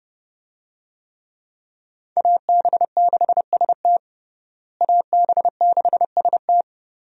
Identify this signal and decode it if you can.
Morse code, heard as a keyed tone, about 32 wpm, 700 Hz. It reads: AB6ST AB6ST
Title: USA Call Signs - 5 Characters - Repeated 3x - 30wpm